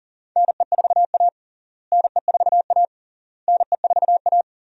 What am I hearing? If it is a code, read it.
Morse code, heard as a keyed tone, about 37 wpm, 700 Hz. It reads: NE4A NE4A NE4A
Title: USA Call Signs - 4 Characters - Repeated 3x - 40wpm